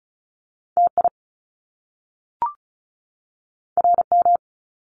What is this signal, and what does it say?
Morse code, heard as a keyed tone, about 35 wpm, 700 Hz. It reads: TI RM
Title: Top Two Letter Combinations in Words 35wpm